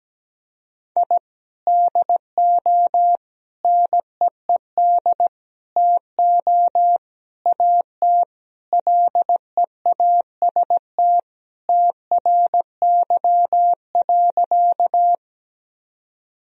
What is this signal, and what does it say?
Morse code, heard as a keyed tone, about 17 wpm, 700 Hz. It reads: I DO NEED TO AT LEAST TRY.
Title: Sentences from Top 1000 Words - Review Part 1 - 17wpm